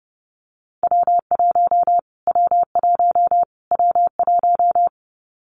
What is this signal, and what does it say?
Morse code, heard as a keyed tone, about 30 wpm, 700 Hz. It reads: W1 W1 W1
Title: Callsign Prefixes for Top 25 DXCC - Repeated 3x - 30wpm